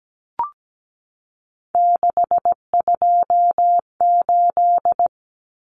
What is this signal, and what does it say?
Morse code, heard as a keyed tone, about 17 wpm, 690 Hz. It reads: 628
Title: Common Cut Numbers - 3-Digit 17wpm